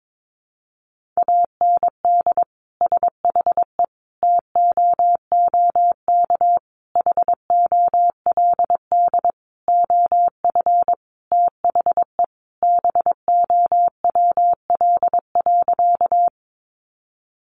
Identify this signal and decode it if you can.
Morse code, heard as a keyed tone, about 22 wpm, 700 Hz. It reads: AND SHE TOOK HOLD OF THE BOWL.
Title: Sentences from Top 1200 Words 22wpm